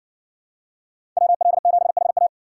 Morse code, heard as a keyed tone, about 60 wpm, 700 Hz. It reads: WL7HA